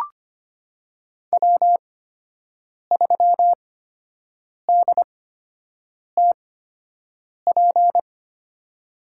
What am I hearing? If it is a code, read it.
Morse code, heard as a keyed tone, about 25 wpm, spaced as for 7 wpm, 700 Hz. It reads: W3DTP
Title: USA Call Signs - 5 Characters - 8x Character Spacing - 25wpm